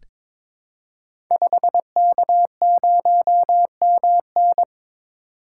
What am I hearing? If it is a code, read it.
Morse code, heard as a keyed tone, about 23 wpm, 700 Hz. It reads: 5K0MN
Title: Code Groups - 5 Letter:Numbers - 22wpm